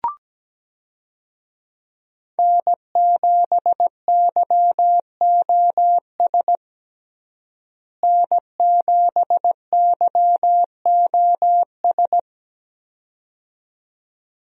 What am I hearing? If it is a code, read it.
Morse code, heard as a keyed tone, about 17 wpm, 700 Hz. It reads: N7YOS N7YOS
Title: USA Call Signs - 5 Characters - Repeated 2x - 17wpm